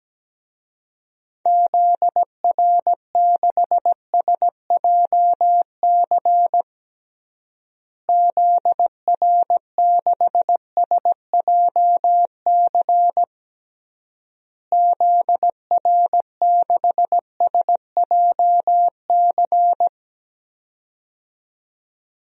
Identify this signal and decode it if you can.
Morse code, heard as a keyed tone, about 17 wpm, 700 Hz. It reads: ZR6SJC ZR6SJC ZR6SJC
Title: DXCC-100 Call Signs - 6 Characters - Repeated 3x - 17wpm